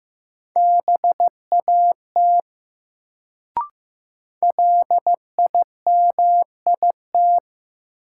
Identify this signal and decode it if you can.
Morse code, heard as a keyed tone, about 15 wpm, 700 Hz. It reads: BAT LIMIT
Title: Sentences from Top 1500 Words - New Word Drill - 15wpm